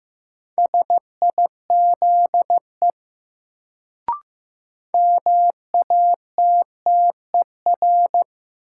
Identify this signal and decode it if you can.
Morse code, heard as a keyed tone, about 15 wpm, 700 Hz. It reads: SIZE MATTER
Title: Sets of 3 Words - New Word Drill - 15wpm